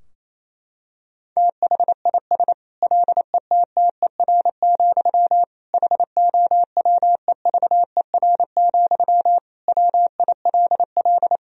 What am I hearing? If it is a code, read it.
Morse code, heard as a keyed tone, about 28 wpm, 700 Hz. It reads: THIS LETTER, HOWEVER, WILL
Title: Sentences from Top 1000 Words - Review Part 4 - 28wpm